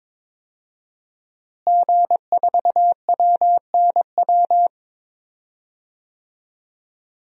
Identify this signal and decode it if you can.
Morse code, heard as a keyed tone, about 21 wpm, 700 Hz. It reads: G4WNW
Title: DXCC-100 Call Signs - 5 Characters - Repeated 2x - 22wpm